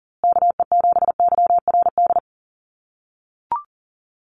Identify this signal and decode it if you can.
Morse code, heard as a keyed tone, about 40 wpm, 700 Hz. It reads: KE7YRD
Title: USA Call Signs Repeated 2x 40wpm